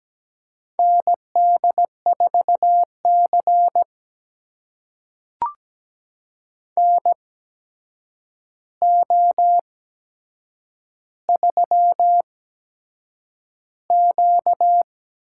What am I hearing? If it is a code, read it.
Morse code, heard as a keyed tone, about 17 wpm, spaced as for 7 wpm, 700 Hz. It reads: ND4C NO3Q
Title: USA Call Signs - 4 Characters - 8x Character Spacing - 17wpm